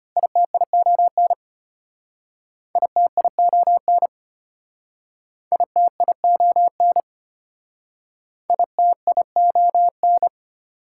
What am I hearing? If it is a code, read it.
Morse code, heard as a keyed tone, about 32 wpm, 700 Hz. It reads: ITION ITION ITION ITION